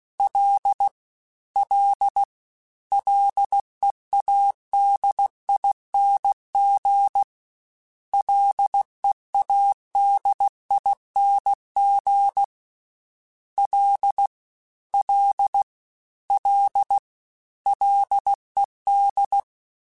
Morse code, heard as a keyed tone, about 16 wpm, 800 Hz. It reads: L L LEADING LEADING L L L LED
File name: Ls / Single L m3s / Single L Prefixe 3x2 Twice @15